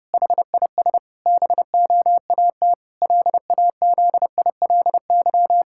Morse code, heard as a keyed tone, about 32 wpm, 700 Hz. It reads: HIS BOAT LAZILY
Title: Pangram Challenge v2 - 30wpm